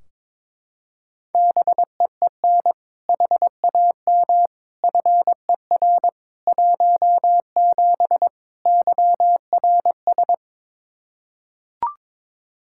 Morse code, heard as a keyed tone, about 22 wpm, 700 Hz. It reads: BEEN HAM FER 17 YRS
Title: Practice QSO - Exchange 3 - More Info - 22wpm